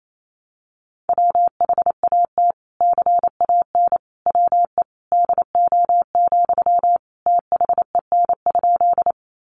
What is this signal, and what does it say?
Morse code, heard as a keyed tone, about 28 wpm, 700 Hz. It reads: WHAT CAN WE DO, THEN?
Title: Sentences from Top 100 Words 28wpm